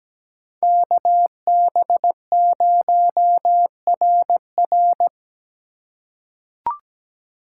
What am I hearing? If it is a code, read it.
Morse code, heard as a keyed tone, about 17 wpm, 700 Hz. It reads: KB0RR
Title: USA Call Signs - 5 Characters - 2x Character Spacing - 17wpm